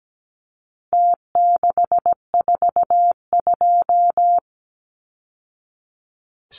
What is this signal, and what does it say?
Morse code, heard as a keyed tone, about 17 wpm, 700 Hz. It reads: T642
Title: Numbers - Contest Style - 4-Digit 17wpm